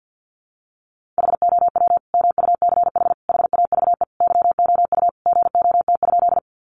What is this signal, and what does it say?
Morse code, heard as a keyed tone, about 49 wpm, 700 Hz. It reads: HOW MUCH HAVE YOU GOT?
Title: Sentences from Top 300 Words 50wpm